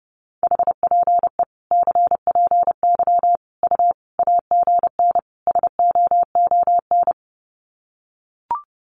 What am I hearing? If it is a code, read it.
Morse code, heard as a keyed tone, about 30 wpm, 700 Hz. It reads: HPE CPY U AGN SOON
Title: Common QSO Phrases 30wpm